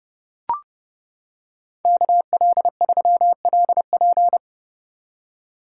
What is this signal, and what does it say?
Morse code, heard as a keyed tone, about 29 wpm, 700 Hz. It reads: KL3LP